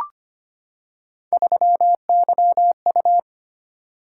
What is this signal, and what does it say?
Morse code, heard as a keyed tone, about 25 wpm, 700 Hz. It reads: 3YU
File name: Three Letter-Number Combinations - 25wpm